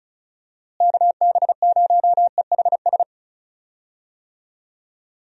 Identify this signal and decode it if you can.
Morse code, heard as a keyed tone, about 36 wpm, 700 Hz. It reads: KB0EHS
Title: USA Call Signs - 6 Characters - Repeated 2x - 35wpm